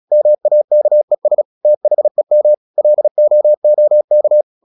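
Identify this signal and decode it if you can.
Morse code, heard as a keyed tone, about 36 wpm, 600 Hz. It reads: MAKES THEM LOOK
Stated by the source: Morse practice at 36 wpm, text of Ken Follett's novel Never, from CW med Gnister.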